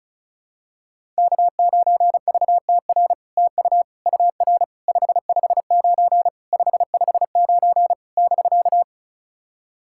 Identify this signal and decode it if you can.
Morse code, heard as a keyed tone, about 36 wpm, 700 Hz. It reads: K9VTR TU UR 559 559 <BK>
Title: SOTA QSO Practice 35wpm